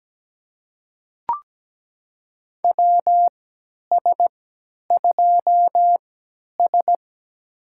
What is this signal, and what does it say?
Morse code, heard as a keyed tone, about 17 wpm, spaced as for 10 wpm, 700 Hz. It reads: WS2S
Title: USA Call Signs - 4 Characters - 3x Character Spacing - 17wpm